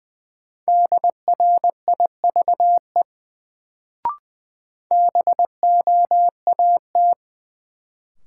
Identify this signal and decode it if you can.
Morse code, heard as a keyed tone, about 20 wpm, 700 Hz. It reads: DRIVE BOAT
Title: Sentences from Top 500 Words - New Word Drill - 20wpm